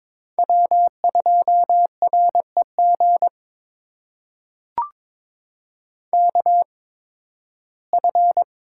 Morse code, heard as a keyed tone, about 22 wpm, spaced as for 14 wpm, 700 Hz. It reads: W2REG KF
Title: USA Call Signs - 5 Characters - 8x Character Spacing - 22wpm